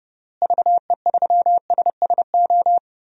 Morse code, heard as a keyed tone, about 30 wpm, 700 Hz. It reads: VE3SSO
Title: DXCC-100 Call Signs - 6 Characters - Repeated 3x - 30wpm